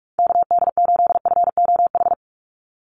Morse code, heard as a keyed tone, about 44 wpm, 690 Hz. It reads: KD8FOH
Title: USA Call Signs 45wpm